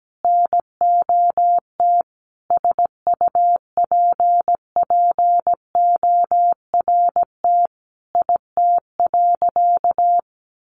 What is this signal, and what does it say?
Morse code, heard as a keyed tone, about 17 wpm, 700 Hz. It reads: NOT SUPPORT IT.
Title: Sentences from Top 1300 Words 17wpm